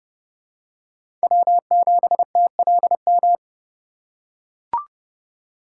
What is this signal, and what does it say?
Morse code, heard as a keyed tone, about 30 wpm, 700 Hz. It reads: W7TLM
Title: USA Call Signs - 5 Characters - 4x Character Spacing - 30wpm